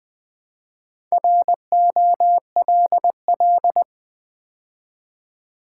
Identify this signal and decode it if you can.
Morse code, heard as a keyed tone, about 20 wpm, 700 Hz. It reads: ROLL
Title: Build A Word - 2 Letters At A Time - 4-Letter Words - 20wpm